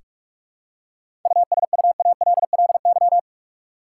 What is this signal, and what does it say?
Morse code, heard as a keyed tone, about 46 wpm, 700 Hz. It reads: USUALLY
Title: Seven Letter Words - 45wpm